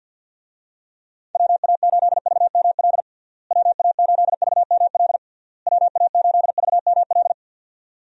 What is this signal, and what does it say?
Morse code, heard as a keyed tone, about 50 wpm, 700 Hz. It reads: WA8VML WA8VML WA8VML
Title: USA Call Signs - 6 Characters - Repeated 3x - 50wpm